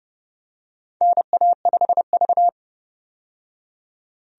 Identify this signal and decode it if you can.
Morse code, heard as a keyed tone, about 30 wpm, 700 Hz. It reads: NA5V
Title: USA Call Signs - 4 Characters - 30wpm